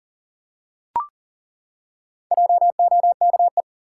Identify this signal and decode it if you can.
Morse code, heard as a keyed tone, about 40 wpm, 700 Hz. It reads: JOKE
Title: Four Letter Words - 40wpm